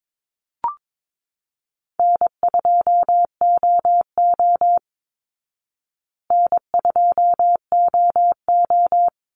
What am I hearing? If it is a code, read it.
Morse code, heard as a keyed tone, about 22 wpm, 690 Hz. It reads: N2OO N2OO
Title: Easy Call Signs Repeated 2x - 22wpm